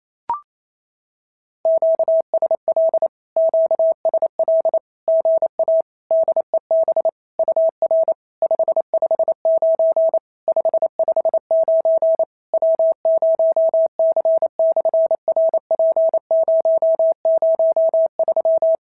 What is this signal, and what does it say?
Morse code, heard as a keyed tone, about 28 wpm, 650 Hz. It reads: QSL QSL GA DEB UR 559 559 W0C/RP003
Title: SOTA QSO Practice 28wpm